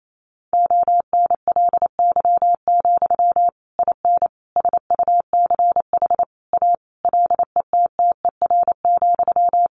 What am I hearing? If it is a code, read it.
Morse code, heard as a keyed tone, about 28 wpm, 700 Hz. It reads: ONLY, IN SUCH A LETTER,
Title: Sentences from Top 1500 Words 28wpm